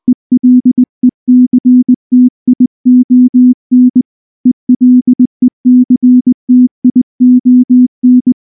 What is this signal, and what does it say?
Morse code, heard as a keyed tone, about 19 wpm, 250 Hz. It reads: ELECTION ELECTION
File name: morse / morse-training-20WPM-5